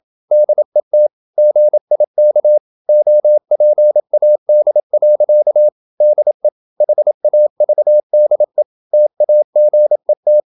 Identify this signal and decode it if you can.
Morse code, heard as a keyed tone, about 26 wpm, 600 Hz. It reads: DET GIK OPAD. DE HAVDE TAGET
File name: Ferie QRQ 27wpm